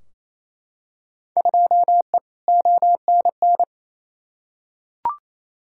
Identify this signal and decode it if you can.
Morse code, heard as a keyed tone, about 28 wpm, 700 Hz. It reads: 2E ONN
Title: Field Day Exchanges Encore - 28wpm